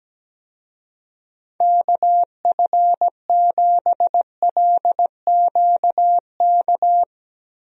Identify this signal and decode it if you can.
Morse code, heard as a keyed tone, about 17 wpm, 700 Hz. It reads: KF7LQK
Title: USA Call Signs - 6 Characters - Repeated 3x - 17wpm